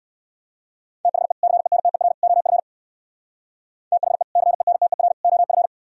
Morse code, heard as a keyed tone, about 64 wpm, 700 Hz. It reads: THE CENTER OF THE CENTER OF